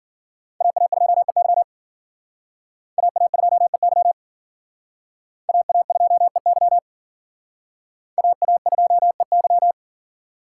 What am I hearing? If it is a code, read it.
Morse code, heard as a keyed tone, about 52 wpm, 700 Hz. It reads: AA2EY AA2EY AA2EY AA2EY